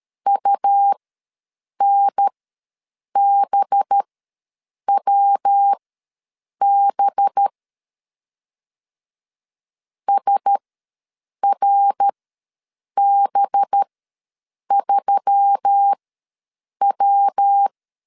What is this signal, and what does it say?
Morse code, heard as a keyed tone, about 13 wpm, spaced as for 7 wpm, 800 Hz. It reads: UNBWB SRB3W
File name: 5 LB ws 12wpm